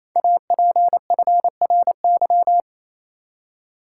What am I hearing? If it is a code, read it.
Morse code, heard as a keyed tone, about 27 wpm, 700 Hz. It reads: APFRY
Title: Code Groups - 5 Letters - 28wpm